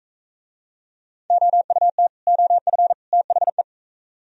Morse code, heard as a keyed tone, about 43 wpm, 700 Hz. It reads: OUT OF THE